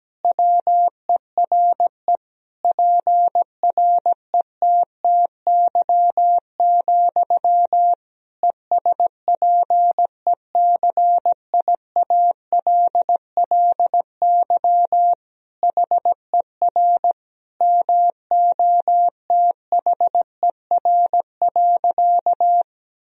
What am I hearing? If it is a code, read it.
Morse code, heard as a keyed tone, about 17 wpm, 700 Hz. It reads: WERE PRETTY, ESPECIALLY HER MOTHER.
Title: Sentences from Top 1000 Words - Review Part 4 - 17wpm